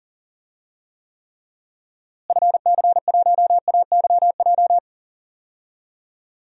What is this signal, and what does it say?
Morse code, heard as a keyed tone, about 40 wpm, 700 Hz. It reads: FC1AYJ
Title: DXCC-100 Call Signs - 6 Characters - Repeated 3x - 40wpm